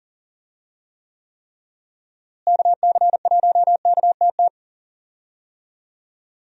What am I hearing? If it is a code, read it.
Morse code, heard as a keyed tone, about 40 wpm, 700 Hz. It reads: KC1KTT
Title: USA Call Signs - 6 Characters - Repeated 3x - 40wpm